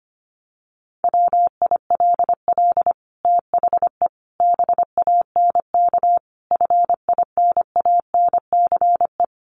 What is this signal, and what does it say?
Morse code, heard as a keyed tone, about 25 wpm, 700 Hz. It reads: WILL THE BANK FINANCE
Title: Sentences from Top 1500 Words 25wpm